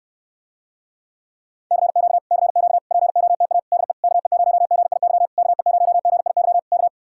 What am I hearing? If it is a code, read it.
Morse code, heard as a keyed tone, about 66 wpm, 700 Hz. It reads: CQ CQ POTA DE KE0GIY KE0GIY K